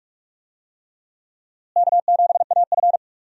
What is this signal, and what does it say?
Morse code, heard as a keyed tone, about 45 wpm, 700 Hz. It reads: K7AF
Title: USA Call Signs - 4 Characters - Repeated 3x - 45wpm